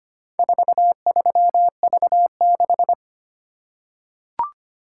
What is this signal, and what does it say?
Morse code, heard as a keyed tone, about 25 wpm, 700 Hz. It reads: 43V6
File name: practice / 25wpm / Four Letter-Number Combinations - 25wpm